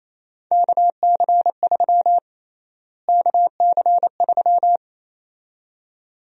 Morse code, heard as a keyed tone, about 28 wpm, 700 Hz. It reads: KC3 KC3
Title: USA Call Sign Prefixes - Repeated 2x - 28wpm